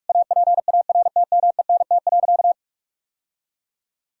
Morse code, heard as a keyed tone, about 46 wpm, 700 Hz. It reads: APARTMENT.
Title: Sentences from Top 1000 Words - Review Part 6 - 45wpm